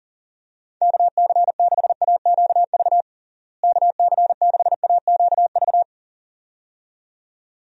Morse code, heard as a keyed tone, about 37 wpm, 700 Hz. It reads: KC6AQV KC6AQV
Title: USA Call Signs - 6 Characters - Repeated 2x - 40wpm